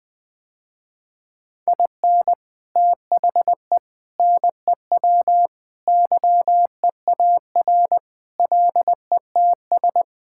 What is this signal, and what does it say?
Morse code, heard as a keyed tone, about 20 wpm, 700 Hz. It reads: IN THE NEW YEAR LETS
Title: Sentences with 3-Word Phrases - Spoken Beginning - 20wpm